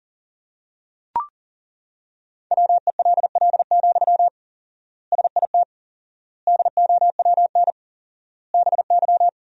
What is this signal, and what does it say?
Morse code, heard as a keyed tone, about 40 wpm, 700 Hz. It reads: WELL, SIT DOWN BY
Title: 4x Word-Spacing - Sentences from Top 500 Words - 40wpm